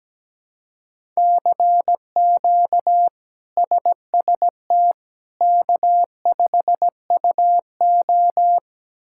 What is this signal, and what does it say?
Morse code, heard as a keyed tone, about 17 wpm, 700 Hz. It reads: CQ SST K5UO
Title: K1USN SST QSO Practice - 17wpm